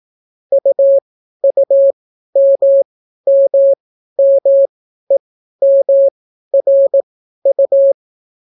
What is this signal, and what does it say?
Morse code, heard as a keyed tone, about 18 wpm, spaced as for 12 wpm, 550 Hz. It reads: UUMMMEMRU